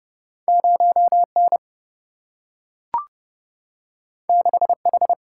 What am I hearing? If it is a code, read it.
Morse code, heard as a keyed tone, about 30 wpm, 700 Hz. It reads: 0N 6H
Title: Single-Letter Single-Number 30wpm